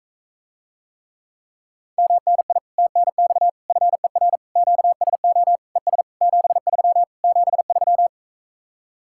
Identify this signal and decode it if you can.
Morse code, heard as a keyed tone, about 44 wpm, 700 Hz. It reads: MNI TNX FER QSO ES 73 73